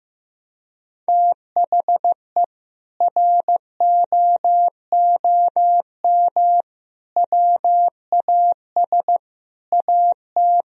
Morse code, heard as a keyed tone, about 15 wpm, 700 Hz. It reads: THE ROOM WAS AT
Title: Sentences from Top 1000 Words - Review Part 2 - 15wpm